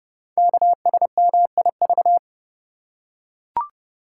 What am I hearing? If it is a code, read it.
Morse code, heard as a keyed tone, about 30 wpm, 700 Hz. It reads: KSMIV
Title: Code Groups - 5 Letters - 30wpm